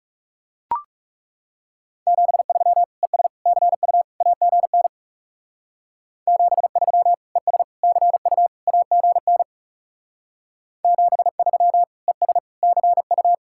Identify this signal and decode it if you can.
Morse code, heard as a keyed tone, about 35 wpm, 700 Hz. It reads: 73 ES CU AGN 73 ES CU AGN 73 ES CU
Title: Speed-Racing Common QSO Phrases - 30wpm